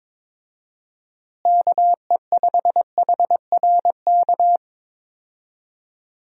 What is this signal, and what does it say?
Morse code, heard as a keyed tone, about 21 wpm, 700 Hz. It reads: KE5HRK
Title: USA Call Signs - 6 Characters - Repeated 2x - 22wpm